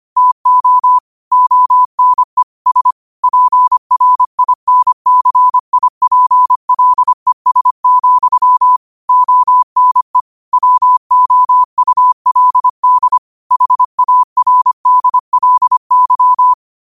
Morse code, heard as a keyed tone, about 25 wpm, 1000 Hz. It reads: TO ONE S PRINCIPLES, ONE WOULD HARDLY
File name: qotdfiles / 25wpm